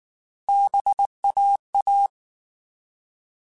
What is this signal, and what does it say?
Morse code, heard as a keyed tone, about 19 wpm, 800 Hz. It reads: BAA
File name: Alphabetical / Alphabetical mp3s / Bs @18